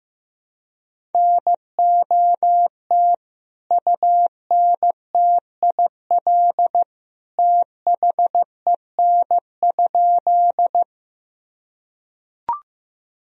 Morse code, heard as a keyed tone, about 15 wpm, 700 Hz. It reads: NOT UNTIL THEN?
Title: Sentences from Top 1000 Words - Review Part 1 - 15wpm